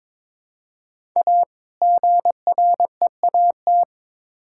Morse code, heard as a keyed tone, about 22 wpm, 700 Hz. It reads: A GREAT
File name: Build A Phrase - 2-Word Phrases - 22wpm